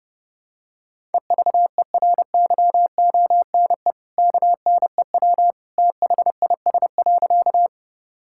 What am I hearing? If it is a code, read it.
Morse code, heard as a keyed tone, about 32 wpm, 700 Hz. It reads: EVERYONE KNEW THIS.
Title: Sentences from Top 1000 Words - Review Part 5 - 30wpm